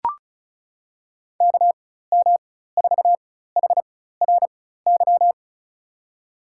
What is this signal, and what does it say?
Morse code, heard as a keyed tone, about 36 wpm, spaced as for 16 wpm, 700 Hz. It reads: KM4HRY
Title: USA Call Signs - 6 Characters - 4x Character Spacing - 35wpm